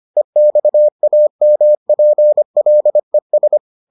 Morse code, heard as a keyed tone, about 25 wpm, 600 Hz. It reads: EXAMPLES